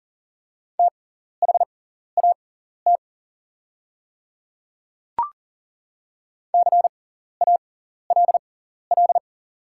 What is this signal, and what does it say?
Morse code, heard as a keyed tone, about 40 wpm, spaced as for 14 wpm, 690 Hz. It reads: THAT CALL
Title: 6x Character Spacing - Top 100 Words - 40wpm